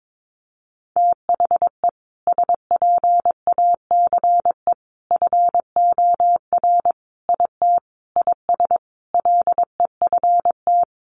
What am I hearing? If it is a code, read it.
Morse code, heard as a keyed tone, about 22 wpm, 700 Hz. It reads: THE SPACE FOR IT IS LEFT
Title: Sentences from Top 1500 Words 22wpm